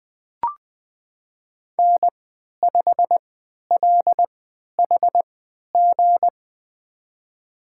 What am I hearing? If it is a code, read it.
Morse code, heard as a keyed tone, about 20 wpm, spaced as for 11 wpm, 700 Hz. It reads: N5LHG
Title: USA Call Signs - 5 Characters - 3x Character Spacing - 20wpm